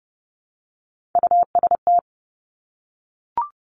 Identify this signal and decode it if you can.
Morse code, heard as a keyed tone, about 30 wpm, 700 Hz. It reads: UST
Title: Top Three Letter Combinations in Words 30wpm